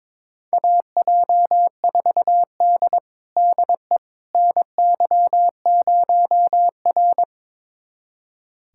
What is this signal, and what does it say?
Morse code, heard as a keyed tone, about 22 wpm, 700 Hz. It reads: AJ4D DE NY0R
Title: USA Callsign DE Callsign - 4 Characters - 22wpm